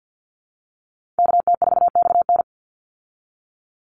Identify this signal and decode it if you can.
Morse code, heard as a keyed tone, about 50 wpm, 700 Hz. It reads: KT4XN
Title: Call Signs Repeated 3x - 50wpm